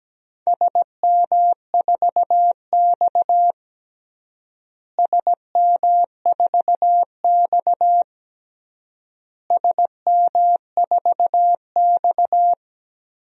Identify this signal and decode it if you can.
Morse code, heard as a keyed tone, about 17 wpm, 700 Hz. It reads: SM4X SM4X SM4X
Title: DXCC-100 Call Signs - 4 Characters - Repeated 3x - 17wpm